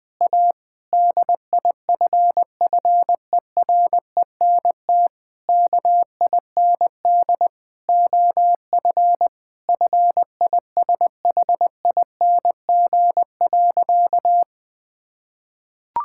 Morse code, heard as a keyed tone, about 20 wpm, 700 Hz. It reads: A DIFFERENT KIND OF FISHING.
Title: Sentences from Top 1000 Words - Review Part 5 - 20wpm